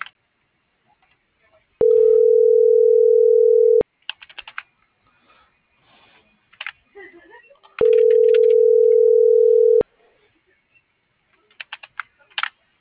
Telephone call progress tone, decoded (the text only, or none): ringback tone